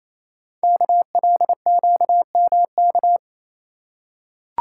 Morse code, heard as a keyed tone, about 29 wpm, 700 Hz. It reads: KLQMK
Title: Code Groups - 5 Letters - 28wpm